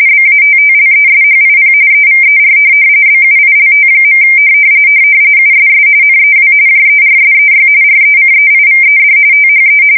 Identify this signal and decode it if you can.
RTTY, about 45.45 baud. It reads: BATTLESHIP EVER BUILT. USS NEW JERSEY RECEIVED THE
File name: fsk170 / USS-New-Jersey